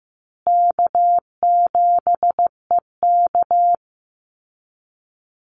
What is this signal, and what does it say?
Morse code, heard as a keyed tone, about 15 wpm, 700 Hz. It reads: K7EK
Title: Easy Call Signs Repeated 3x - 15wpm